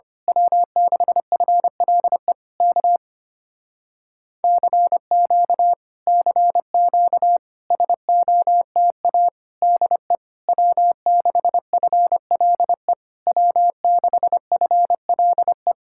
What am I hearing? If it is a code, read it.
Morse code, heard as a keyed tone, about 24 wpm, 700 Hz. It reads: W6FLE K CQ CQ SOTA DE W6FLE W6FLE
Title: Speed-Racing Common QSO Phrases - 25wpm